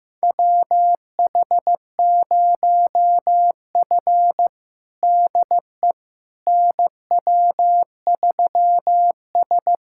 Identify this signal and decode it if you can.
Morse code, heard as a keyed tone, about 15 wpm, 700 Hz. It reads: WH0F DE NW3S
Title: USA Callsign DE Callsign - 4 Characters - 15wpm